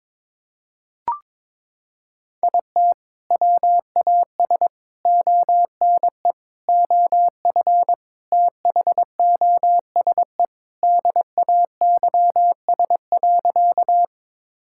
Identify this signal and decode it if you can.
Morse code, heard as a keyed tone, about 22 wpm, 700 Hz. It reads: IT WAS ONE OF THOSE DAYS.
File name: practice / 22wpm / Sentences with 4-Word Phrases - 22wpm